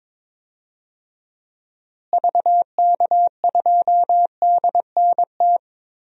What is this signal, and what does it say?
Morse code, heard as a keyed tone, about 22 wpm, 700 Hz. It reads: VK2DNT
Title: DXCC-100 Call Signs - 6 Characters - Repeated 3x - 22wpm